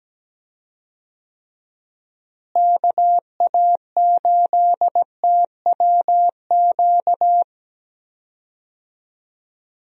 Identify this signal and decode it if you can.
Morse code, heard as a keyed tone, about 17 wpm, 700 Hz. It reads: KA8TWQ
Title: USA Call Signs - 6 Characters - Repeated 3x - 17wpm